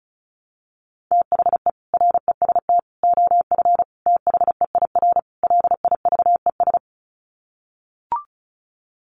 Morse code, heard as a keyed tone, about 34 wpm, 700 Hz. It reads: THE REST OF THEIR LIVES
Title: Sets of 5 Words Encore 35wpm